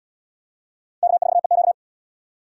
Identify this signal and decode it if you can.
Morse code, heard as a keyed tone, about 78 wpm, 700 Hz. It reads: K4EQ